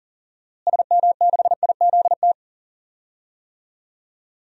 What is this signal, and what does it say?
Morse code, heard as a keyed tone, about 37 wpm, 700 Hz. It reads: SM6IZT
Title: DXCC-100 Call Signs - 6 Characters - Repeated 3x - 40wpm